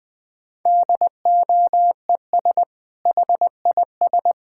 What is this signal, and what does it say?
Morse code, heard as a keyed tone, about 21 wpm, 700 Hz. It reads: DOES HIS
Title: Sentences from Top 1000 Words - Review Part 6 - 20wpm